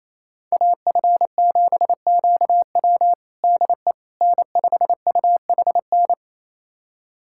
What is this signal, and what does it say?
Morse code, heard as a keyed tone, about 28 wpm, 700 Hz. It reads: AF7QW DE N5UHN
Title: USA Callsign DE Callsign - 5 Characters - 28wpm